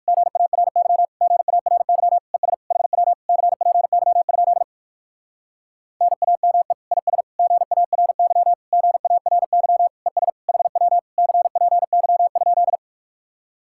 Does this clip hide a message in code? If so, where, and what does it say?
Yes: Morse code from 0.1 to 12.8 s, heard as a keyed tone, about 59 wpm, 700 Hz. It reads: GARY GARY ES HW CPY? NAME IS GARY GARY ES HW CPY?